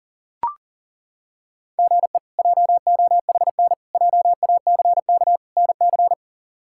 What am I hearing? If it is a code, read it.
Morse code, heard as a keyed tone, about 40 wpm, 700 Hz. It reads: GE JOHN JACK NC
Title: K1USN SST QSO Practice - 40wpm